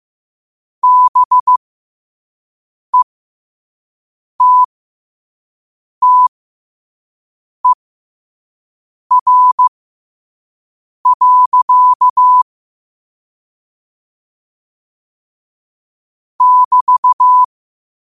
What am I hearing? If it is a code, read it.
Morse code, heard as a keyed tone, about 15 wpm, spaced as for 5 wpm, 1000 Hz. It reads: BETTER. <BT>
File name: qotdfiles / 5wpm